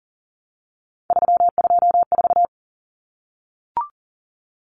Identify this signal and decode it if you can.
Morse code, heard as a keyed tone, about 37 wpm, 700 Hz. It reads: 324
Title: Numbers - 3-Digit 40wpm